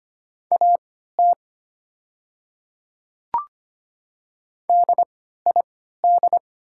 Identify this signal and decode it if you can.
Morse code, heard as a keyed tone, about 25 wpm, spaced as for 14 wpm, 700 Hz. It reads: AT DID
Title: 3x Character Spacing - Top 100 Words - 25wpm